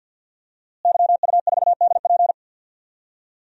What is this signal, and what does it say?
Morse code, heard as a keyed tone, about 47 wpm, 700 Hz. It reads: YU4DP
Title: DXCC-100 Call Signs - 5 Characters - Repeated 2x - 50wpm